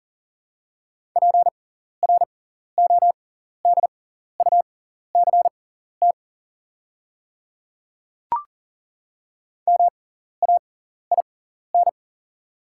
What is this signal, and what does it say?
Morse code, heard as a keyed tone, about 40 wpm, spaced as for 14 wpm, 700 Hz. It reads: PRODUCT MAIN
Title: 6x Character Spacing - Top 500 Words - 40wpm